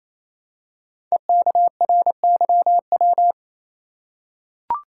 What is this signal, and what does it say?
Morse code, heard as a keyed tone, about 27 wpm, 700 Hz. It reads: EKRYW
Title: Code Groups - 5 Letters - 28wpm